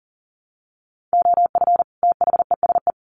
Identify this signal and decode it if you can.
Morse code, heard as a keyed tone, about 40 wpm, 700 Hz. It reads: OF THESE
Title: Sets of 2 Words 40wpm